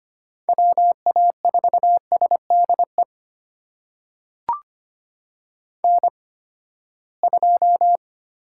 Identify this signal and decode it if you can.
Morse code, heard as a keyed tone, about 25 wpm, spaced as for 17 wpm, 700 Hz. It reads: WA4SDE N2
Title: USA Call Signs - 8x Character Spacing - 25wpm